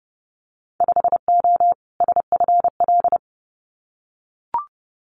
Morse code, heard as a keyed tone, about 30 wpm, 700 Hz. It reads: 5O SFL
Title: Winter Field Day Exchanges 30wpm